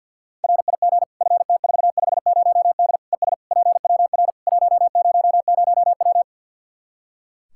Morse code, heard as a keyed tone, about 52 wpm, 700 Hz. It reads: RIG FT450D ES PWR 100W
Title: Practice QSO - Exchange 2 - Info - 50wpm